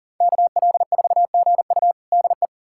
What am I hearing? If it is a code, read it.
Morse code, heard as a keyed tone, about 40 wpm, 700 Hz. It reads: KL4GU DE
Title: USA Callsign DE Callsign - 5 Characters - 40wpm